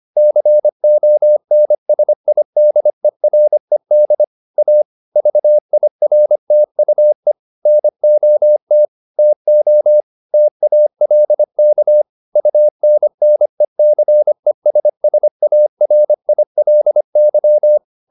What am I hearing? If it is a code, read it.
Morse code, heard as a keyed tone, about 25 wpm, 600 Hz. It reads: CONSIDERED A VIRTUE NOT TO TALK UNNECESSARILY
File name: the old man and the sea part 1 25wpm